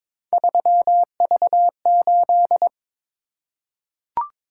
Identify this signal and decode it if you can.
Morse code, heard as a keyed tone, about 22 wpm, 700 Hz. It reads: 3V8
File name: practice / 22wpm / Three Letter-Number Combinations - 22wpm